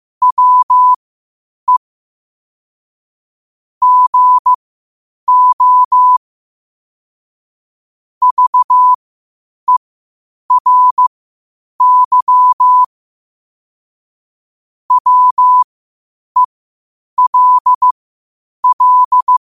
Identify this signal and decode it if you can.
Morse code, heard as a keyed tone, about 15 wpm, spaced as for 8 wpm, 1000 Hz. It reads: WE GO VERY WELL
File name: qotdfiles / 8wpm